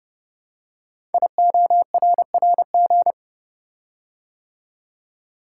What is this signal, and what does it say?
Morse code, heard as a keyed tone, about 30 wpm, 690 Hz. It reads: IORRG
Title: Code Groups - 5 Letters - 30wpm